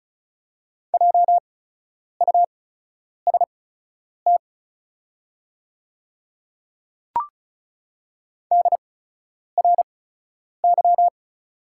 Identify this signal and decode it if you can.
Morse code, heard as a keyed tone, about 36 wpm, spaced as for 10 wpm, 690 Hz. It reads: JUST DRY
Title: 8x Character Spacing - Top 500 Words - 35wpm